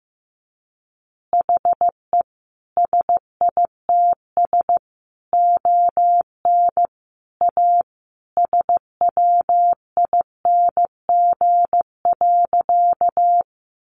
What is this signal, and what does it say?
Morse code, heard as a keyed tone, about 15 wpm, 700 Hz. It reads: HE SITS ON A SWING.
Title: Additional Easy Phrases 15wpm